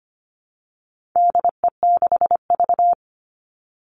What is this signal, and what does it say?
Morse code, heard as a keyed tone, about 25 wpm, 700 Hz. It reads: DE6V
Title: All Cut Numbers - 4-Digit 25wpm